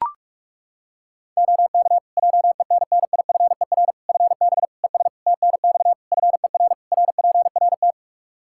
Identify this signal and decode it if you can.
Morse code, heard as a keyed tone, about 43 wpm, 700 Hz. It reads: OK JENNIFER FB ES TNX FER RPRT
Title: Practice QSO - Exchange 2 - Info - 45wpm